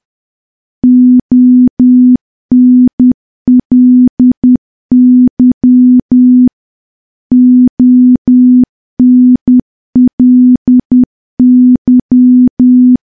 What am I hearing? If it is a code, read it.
Morse code, heard as a keyed tone, about 10 wpm, 250 Hz. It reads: ONLY ONLY
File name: morse / morse-training-10WPM